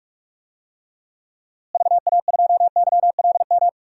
Morse code, heard as a keyed tone, about 45 wpm, 700 Hz. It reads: VA2YLM
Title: DXCC-100 Call Signs - 6 Characters - Repeated 2x - 45wpm